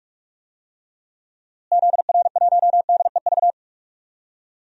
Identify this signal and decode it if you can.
Morse code, heard as a keyed tone, about 46 wpm, 700 Hz. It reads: ZR1DEV
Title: DXCC-100 Call Signs - 6 Characters - Repeated 3x - 45wpm